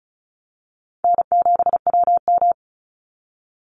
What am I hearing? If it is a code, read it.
Morse code, heard as a keyed tone, about 34 wpm, 700 Hz. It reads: N7WM
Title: NA SOTA Chaser Callsigns 35wpm